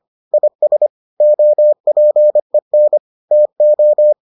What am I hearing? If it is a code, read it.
Morse code, heard as a keyed tone, about 25 wpm, 600 Hz. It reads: IS OPEN TO